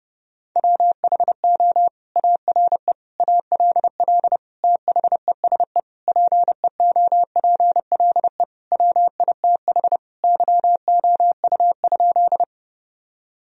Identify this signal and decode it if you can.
Morse code, heard as a keyed tone, about 32 wpm, 700 Hz. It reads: WHO ARE ALL THESE PEOPLE WITH YOU?
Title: Sentences from Top 1000 Words - Review Part 1 - 30wpm